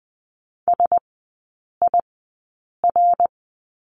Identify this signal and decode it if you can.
Morse code, heard as a keyed tone, about 20 wpm, spaced as for 8 wpm, 700 Hz. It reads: SIR
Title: 053 Lesson 10e - Letter r - Three Characters 20wpm